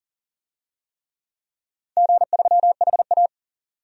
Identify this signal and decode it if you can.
Morse code, heard as a keyed tone, about 39 wpm, 700 Hz. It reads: G3HA
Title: DXCC-100 Call Signs - 4 Characters - 40wpm